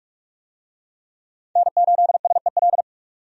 Morse code, heard as a keyed tone, about 47 wpm, 700 Hz. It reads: N8SEL
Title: USA Call Signs - 5 Characters - 45wpm